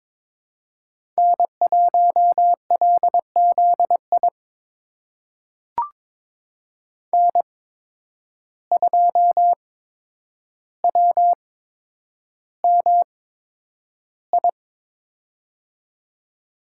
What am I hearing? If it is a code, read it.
Morse code, heard as a keyed tone, about 21 wpm, spaced as for 9 wpm, 700 Hz. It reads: N1LZI N2WMI
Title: USA Call Signs - 5 Characters - 8x Character Spacing - 22wpm